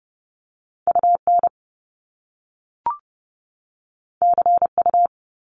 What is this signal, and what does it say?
Morse code, heard as a keyed tone, about 30 wpm, 700 Hz. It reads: UN CU
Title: Top Two Letter Combinations in Words 30wpm